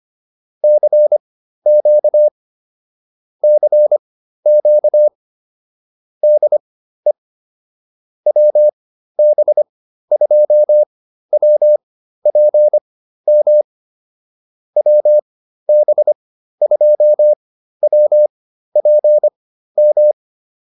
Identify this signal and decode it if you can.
Morse code, heard as a keyed tone, about 25 wpm, spaced as for 13 wpm, 600 Hz. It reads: CQ CQ DE WB2WPM WB2WPM